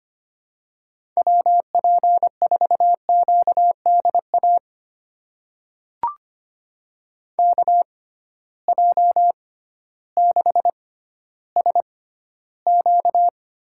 Morse code, heard as a keyed tone, about 25 wpm, spaced as for 14 wpm, 700 Hz. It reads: WP4QDA KJ6SQ
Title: USA Call Signs - 6 Characters - 6x Character Spacing - 25wpm